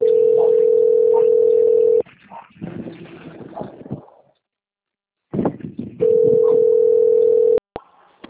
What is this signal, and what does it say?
Ringback tone (North America), 440 + 480 Hz.
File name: out- - - - - .54288